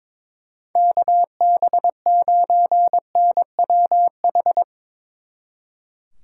Morse code, heard as a keyed tone, about 24 wpm, 700 Hz. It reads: KB9NWH
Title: Speed Racing - USA 2x3 Call Signs 22wpm